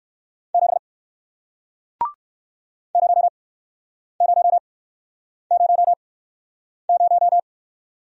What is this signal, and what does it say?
Morse code, heard as a keyed tone, about 57 wpm, spaced as for 9 wpm, 700 Hz. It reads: 7 0000